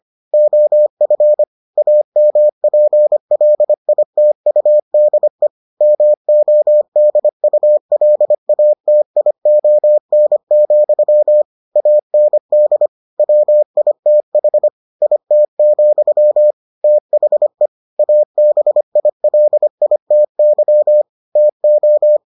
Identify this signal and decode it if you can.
Morse code, heard as a keyed tone, about 25 wpm, 600 Hz. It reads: OF AMPLITUDE MODULATION, AND WITH IT, THE ABILITY TO